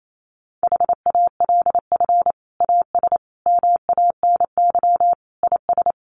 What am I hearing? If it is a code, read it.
Morse code, heard as a keyed tone, about 28 wpm, 700 Hz. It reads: HALF AS MANY IS
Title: Sentences from Top 400 Words 28wpm